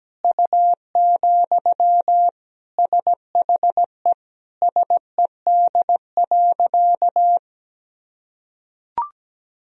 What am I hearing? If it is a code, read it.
Morse code, heard as a keyed tone, about 17 wpm, 700 Hz. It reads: U, SHE SED.
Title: Sentences from Top 400 Words and Abbreviations 17wpm